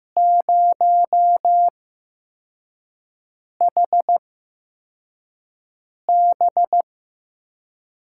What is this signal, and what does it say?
Morse code, heard as a keyed tone, about 15 wpm, spaced as for 4 wpm, 700 Hz. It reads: 0HB
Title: USA Call Signs - 4 Characters - 8x Character Spacing - 15wpm